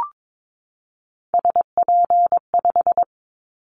Morse code, heard as a keyed tone, about 22 wpm, 700 Hz. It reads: SP5
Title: Callsign Prefixes for Top 25 DXCC - 22wpm